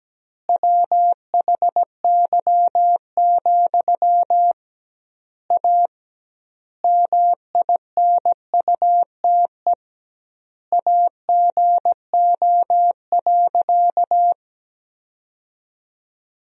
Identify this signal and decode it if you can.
Morse code, heard as a keyed tone, about 17 wpm, 700 Hz. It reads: WHY, A MINUTE AGO.
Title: 2x Word-Spacing - Sentences from Top 500 Words - 17wpm